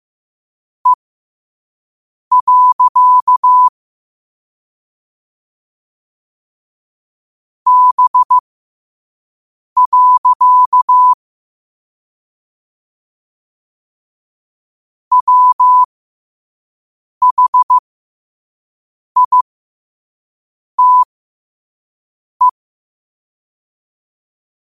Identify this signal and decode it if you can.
Morse code, heard as a keyed tone, about 15 wpm, spaced as for 5 wpm, 1000 Hz. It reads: E. B. WHITE
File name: qotdfiles / 5wpm